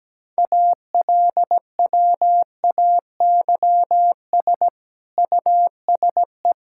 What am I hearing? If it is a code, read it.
Morse code, heard as a keyed tone, about 17 wpm, 700 Hz. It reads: ALWAYS USE
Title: Sentences from Top 1000 Words - Review Part 6 - 17wpm